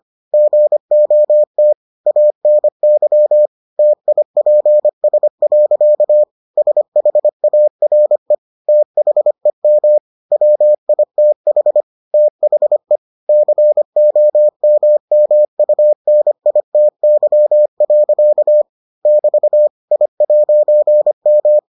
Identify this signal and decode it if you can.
Morse code, heard as a keyed tone, about 25 wpm, 600 Hz. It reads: GOT ANY TIPS. SHARE THEM WITH THE COMMUNITY. <BT> I'M